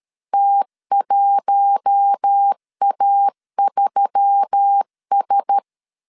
Morse code, heard as a keyed tone, about 13 wpm, 800 Hz. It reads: T1A3S
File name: CW / DET / 1 6 1500C S B 12wpm 3431